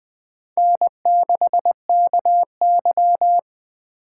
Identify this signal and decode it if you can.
Morse code, heard as a keyed tone, about 20 wpm, 700 Hz. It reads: N6KY
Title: Speed Racing - USA Call Signs 17wpm